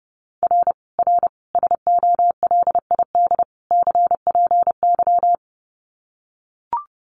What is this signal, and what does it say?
Morse code, heard as a keyed tone, about 30 wpm, 700 Hz. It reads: R R SOLID CPY
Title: Common QSO Phrases 30wpm